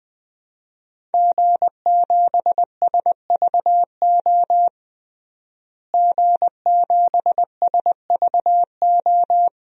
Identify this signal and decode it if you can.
Morse code, heard as a keyed tone, about 20 wpm, 700 Hz. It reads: G7SVO G7SVO
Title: DXCC-100 Call Signs - 5 Characters - Repeated 3x - 20wpm